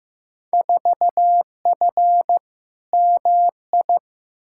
Morse code, heard as a keyed tone, about 15 wpm, 700 Hz. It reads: 4F MI
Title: Field Day Exchanges Encore - 15wpm